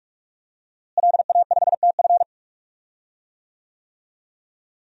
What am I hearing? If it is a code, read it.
Morse code, heard as a keyed tone, about 44 wpm, 700 Hz. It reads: LA5TF
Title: DXCC-100 Call Signs - 5 Characters - Repeated 2x - 45wpm